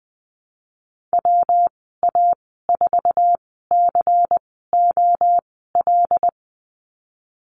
Morse code, heard as a keyed tone, about 20 wpm, spaced as for 15 wpm, 700 Hz. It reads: WA4COL
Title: 122 Lesson 20g - Number 3 - Call Signs 20wpm